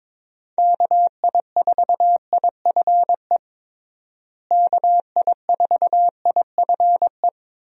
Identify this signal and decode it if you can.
Morse code, heard as a keyed tone, about 22 wpm, 700 Hz. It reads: KI4IFE KI4IFE
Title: USA Call Signs - 6 Characters - Repeated 3x - 22wpm